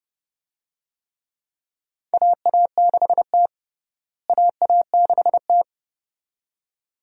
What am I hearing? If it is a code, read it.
Morse code, heard as a keyed tone, about 30 wpm, 700 Hz. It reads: AA6T AA6T
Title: USA Call Signs - 4 Characters - Repeated 2x - 30wpm